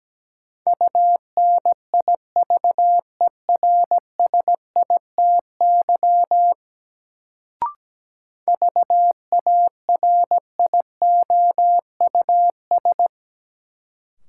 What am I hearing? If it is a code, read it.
Morse code, heard as a keyed tone, about 17 wpm, 700 Hz. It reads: UNIVERSITY VARIOUS
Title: Sentences from Top 1000 Words - Review Part 4 - New Word Drill - 17wpm